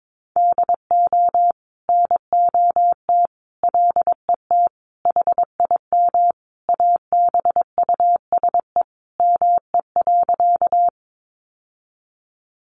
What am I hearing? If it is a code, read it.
Morse code, heard as a keyed tone, about 22 wpm, 700 Hz. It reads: DO NOT LET HIM ABUSE ME.
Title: Sentences from Top 1300 Words 22wpm